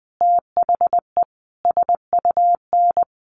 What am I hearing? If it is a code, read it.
Morse code, heard as a keyed tone, about 20 wpm, 700 Hz. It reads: THE SUN
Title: Sentences from Top 400 Words 20wpm